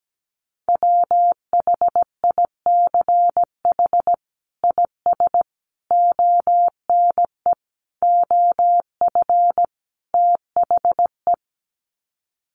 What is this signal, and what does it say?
Morse code, heard as a keyed tone, about 17 wpm, 700 Hz. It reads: WHICH IS ONE OF THE
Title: Sets of 5 Words Encore 17wpm